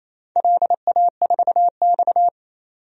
Morse code, heard as a keyed tone, about 28 wpm, 700 Hz. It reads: LA4X
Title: DXCC-100 Call Signs - 4 Characters - 28wpm